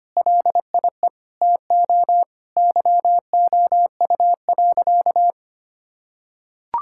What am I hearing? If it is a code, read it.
Morse code, heard as a keyed tone, about 25 wpm, 700 Hz. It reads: LIE TO YOU.
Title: Sentences from Top 1600 Words 25wpm